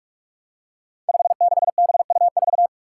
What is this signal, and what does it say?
Morse code, heard as a keyed tone, about 41 wpm, 700 Hz. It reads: 56BU4
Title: Five Challenging Letter-Number Combinations 45wpm